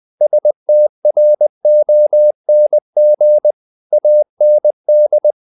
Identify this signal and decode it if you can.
Morse code, heard as a keyed tone, about 20 wpm, 600 Hz. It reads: STRONG AND
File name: the old man and the sea part 2 20wpm